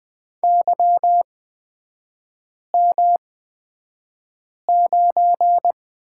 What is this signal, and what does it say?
Morse code, heard as a keyed tone, about 20 wpm, spaced as for 5 wpm, 700 Hz. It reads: YM9
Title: 243 Lesson 39e - Letter z - Three Characters 20wpm